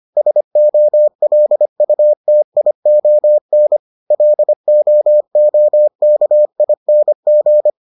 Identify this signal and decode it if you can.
Morse code, heard as a keyed tone, about 25 wpm, 600 Hz. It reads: SOLUTION LOOKING